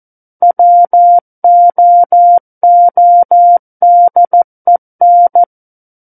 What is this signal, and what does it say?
Morse code, heard as a keyed tone, about 14 wpm, 700 Hz. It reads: WOODEN
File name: Book One / 05-Chapter 5 - 14 WPM